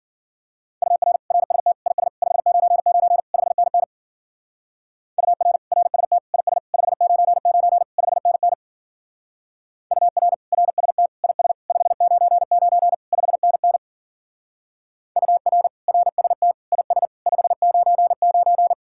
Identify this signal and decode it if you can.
Morse code, heard as a keyed tone, about 48 wpm, 700 Hz. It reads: UR RST IS 599 5NN UR RST IS 599 5NN UR RST IS 599 5NN UR RST IS 599
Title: Speed-Racing Common QSO Phrases Encore - 40wpm